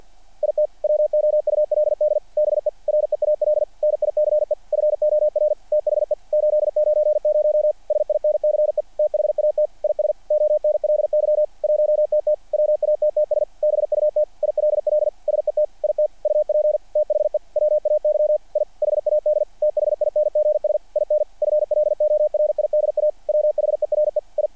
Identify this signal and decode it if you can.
Morse code, heard as a keyed tone, about 48 wpm, 600 Hz. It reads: IT WOULD BE REAL NICE FOR THE 890 SINCE THAT IS ONLY 1TT WATTS BUT ILL SET IT UP THE WAY I HAD THINGS IN FLORIDA WHERE I
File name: archived / CRSnet